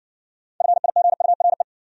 Morse code, heard as a keyed tone, about 60 wpm, 700 Hz. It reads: FIGURE